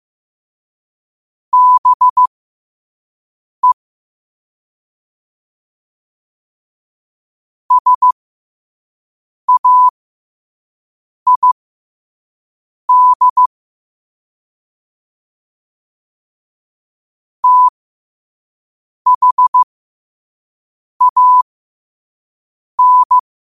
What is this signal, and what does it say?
Morse code, heard as a keyed tone, about 15 wpm, spaced as for 5 wpm, 1000 Hz. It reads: BE SAID THAN
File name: qotdfiles / 5wpm